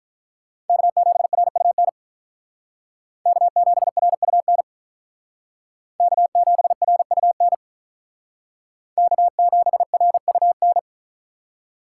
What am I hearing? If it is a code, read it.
Morse code, heard as a keyed tone, about 47 wpm, 700 Hz. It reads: K7RUN K7RUN K7RUN K7RUN